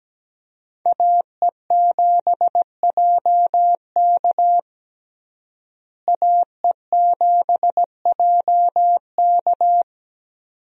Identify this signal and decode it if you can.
Morse code, heard as a keyed tone, about 17 wpm, 700 Hz. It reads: AE7JK AE7JK
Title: USA Call Signs - 5 Characters - Repeated 2x - 17wpm